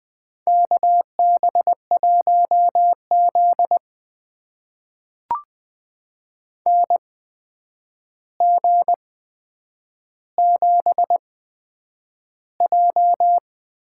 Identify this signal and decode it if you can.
Morse code, heard as a keyed tone, about 20 wpm, spaced as for 9 wpm, 700 Hz. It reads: KB1Z NG7J
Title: USA Call Signs - 8x Character Spacing - 20wpm